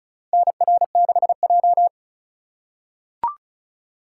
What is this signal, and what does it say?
Morse code, heard as a keyed tone, about 36 wpm, 700 Hz. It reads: NR6J
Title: USA Call Signs - 4 Characters - 3x Character Spacing - 35wpm